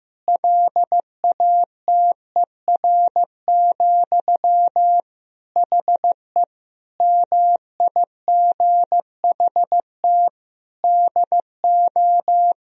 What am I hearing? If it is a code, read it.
Morse code, heard as a keyed tone, about 15 wpm, 700 Hz. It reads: LATER, HE MIGHT DO
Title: Sentences from Top 1000 Words - Review Part 4 - 15wpm